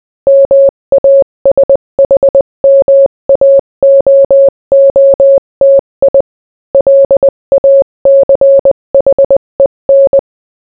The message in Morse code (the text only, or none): MASHMAOOTI LACHEN